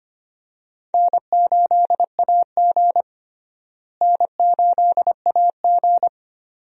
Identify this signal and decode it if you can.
Morse code, heard as a keyed tone, about 25 wpm, 700 Hz. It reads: N8AG N8AG
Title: USA Call Signs - 4 Characters - Repeated 2x - 25wpm